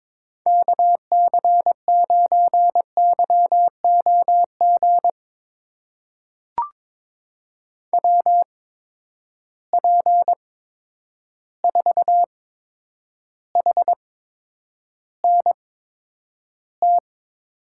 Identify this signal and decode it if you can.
Morse code, heard as a keyed tone, about 22 wpm, spaced as for 9 wpm, 700 Hz. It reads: KC9YOG WP4HNT
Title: USA Call Signs - 6 Characters - 8x Character Spacing - 22wpm